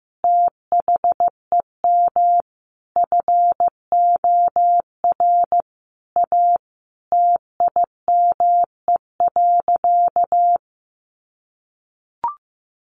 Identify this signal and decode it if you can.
Morse code, heard as a keyed tone, about 15 wpm, 700 Hz. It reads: THEM FOR A TIME.
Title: Sentences from Top 1200 Words 15wpm